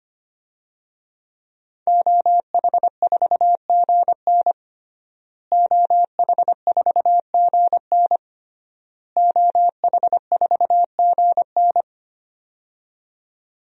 Morse code, heard as a keyed tone, about 25 wpm, 700 Hz. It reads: OH4GN OH4GN OH4GN
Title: DXCC-100 Call Signs - 5 Characters - Repeated 3x - 25wpm